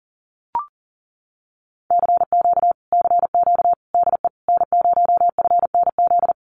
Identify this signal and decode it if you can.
Morse code, heard as a keyed tone, about 40 wpm, 700 Hz. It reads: CQ CQ DE N0FNZ
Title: Common QSO Phrases Encore 40wpm